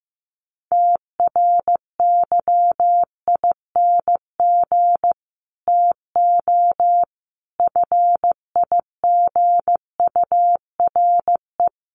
Morse code, heard as a keyed tone, about 15 wpm, 700 Hz. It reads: TRYING TO FIGURE
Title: Sets of 5 Words Encore 15wpm